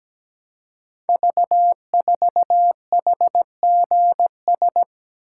Morse code, heard as a keyed tone, about 17 wpm, 700 Hz. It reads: V4HGS
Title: Code Groups - 5 Letter:Numbers - 17wpm